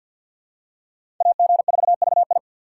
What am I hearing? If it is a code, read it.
Morse code, heard as a keyed tone, about 49 wpm, 700 Hz. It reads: AG4VI
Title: USA Call Signs - 5 Characters - 4x Character Spacing - 50wpm